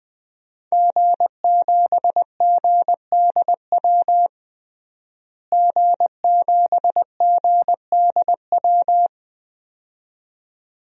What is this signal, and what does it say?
Morse code, heard as a keyed tone, about 20 wpm, 700 Hz. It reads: G7GDW G7GDW
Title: DXCC-100 Call Signs - 5 Characters - Repeated 2x - 20wpm